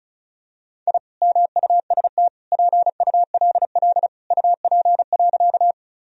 Morse code, heard as a keyed tone, about 36 wpm, 700 Hz. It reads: I MUST PULL UP.
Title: Sentences from Top 1000 Words - Review Part 2 - 35wpm